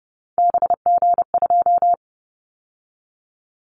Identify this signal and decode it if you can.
Morse code, heard as a keyed tone, about 30 wpm, 700 Hz. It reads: BG2
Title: Callsign Prefixes for Top 100 DXCC - 30wpm